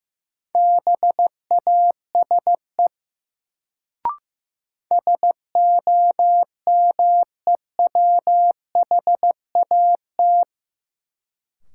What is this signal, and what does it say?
Morse code, heard as a keyed tone, about 15 wpm, 700 Hz. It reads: BASE SOMEWHAT
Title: Sentences from Top 1000 Words - Review Part 1 - New Word Drill - 15wpm